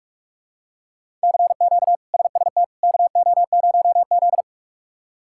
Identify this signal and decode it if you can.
Morse code, heard as a keyed tone, about 45 wpm, 700 Hz. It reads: CQ SST KO0Z
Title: K1USN SST QSO Practice - 45wpm